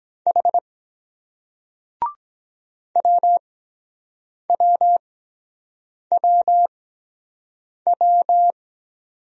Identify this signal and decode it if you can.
Morse code, heard as a keyed tone, about 23 wpm, spaced as for 7 wpm, 700 Hz. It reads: H WWWW